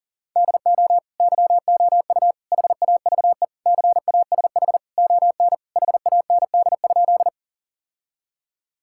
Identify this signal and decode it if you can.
Morse code, heard as a keyed tone, about 40 wpm, 700 Hz. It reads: DO YOU HAVE CASH ON HAND?
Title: Sentences from Top 1000 Words - Review Part 5 - 40wpm